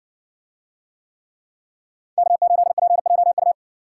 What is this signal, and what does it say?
Morse code, heard as a keyed tone, about 60 wpm, 700 Hz. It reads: K8PJU